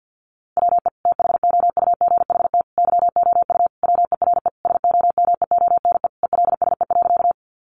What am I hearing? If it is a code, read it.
Morse code, heard as a keyed tone, about 50 wpm, 700 Hz. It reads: WE THOUGHT YOU WERE SOMEONE ELSE.
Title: Sentences from Top 800 Words 50wpm